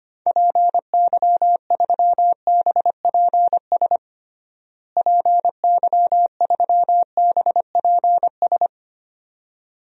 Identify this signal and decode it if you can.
Morse code, heard as a keyed tone, about 25 wpm, 700 Hz. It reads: PY3BPS PY3BPS
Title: DXCC-100 Call Signs - 6 Characters - Repeated 2x - 25wpm